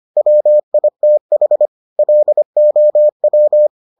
Morse code, heard as a keyed tone, about 25 wpm, 600 Hz. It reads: WITH LOW